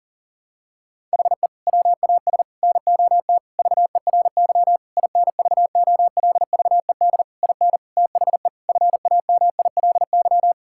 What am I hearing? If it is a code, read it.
Morse code, heard as a keyed tone, about 40 wpm, 700 Hz. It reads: HE WAS NOT VERY INVOLVED IN THE FAMILY
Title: Sentences from Top 1000 Words - Review Part 5 - 40wpm